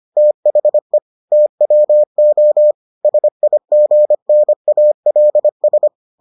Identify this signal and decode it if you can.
Morse code, heard as a keyed tone, about 25 wpm, 600 Hz. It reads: THE TWO SIGNALS